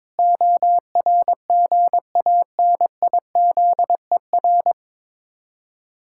Morse code, heard as a keyed tone, about 22 wpm, 700 Hz. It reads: ORGANIZER
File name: Difficult Words with Hints - 22wpm